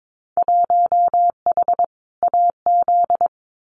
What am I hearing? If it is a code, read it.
Morse code, heard as a keyed tone, about 22 wpm, 700 Hz. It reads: 1H AZ
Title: Winter Field Day Exchanges 22wpm